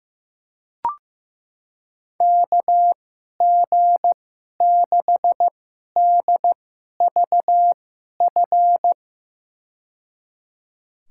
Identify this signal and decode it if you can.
Morse code, heard as a keyed tone, about 15 wpm, spaced as for 11 wpm, 700 Hz. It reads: KG6DVF
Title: USA Call Signs - 6 Characters - 2x Character Spacing - 15wpm